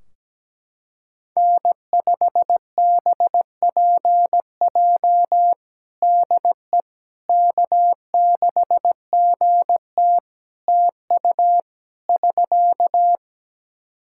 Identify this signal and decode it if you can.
Morse code, heard as a keyed tone, about 17 wpm, 700 Hz. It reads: N5BPJ DE K6GT TU <SK>
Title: K1USN SST QSO Practice 17wpm